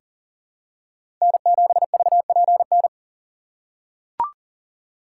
Morse code, heard as a keyed tone, about 40 wpm, 700 Hz. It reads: N7VPN
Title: USA Call Signs - 5 Characters - 2x Character Spacing - 40wpm